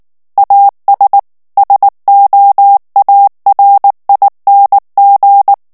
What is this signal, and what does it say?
Morse code, heard as a keyed tone, about 19 wpm, 800 Hz. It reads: AS SOARING
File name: ANGLAIS texte clair 2 vitesse 18 mots minute